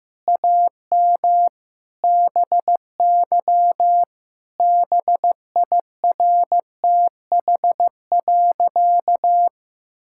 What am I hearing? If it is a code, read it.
Morse code, heard as a keyed tone, about 16 wpm, 700 Hz. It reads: AM BY BIRTH.
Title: Sentences from Top 1000 Words - Review Part 5 - 15wpm